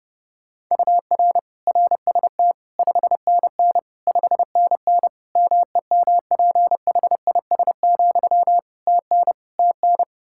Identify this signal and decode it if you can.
Morse code, heard as a keyed tone, about 30 wpm, 700 Hz. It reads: UR RST 5NN 5NN MEMPHIS, TN TN
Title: Practice QSO - Exchange 1 - Beginning - 30wpm